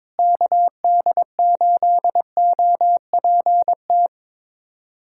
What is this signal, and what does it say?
Morse code, heard as a keyed tone, about 22 wpm, 700 Hz. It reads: KD8OPT
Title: USA Callsign DE Callsign 22wpm